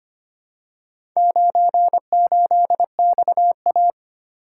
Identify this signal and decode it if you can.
Morse code, heard as a keyed tone, about 25 wpm, 700 Hz. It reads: 98XA
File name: practice / 25wpm / Four Letter-Number Combinations - 25wpm